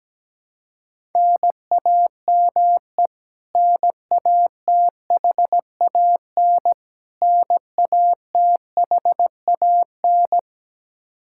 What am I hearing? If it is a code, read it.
Morse code, heard as a keyed tone, about 17 wpm, 700 Hz. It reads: NAME NATHAN NATHAN
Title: Practice QSO - Exchange 1 - Beginning - 17wpm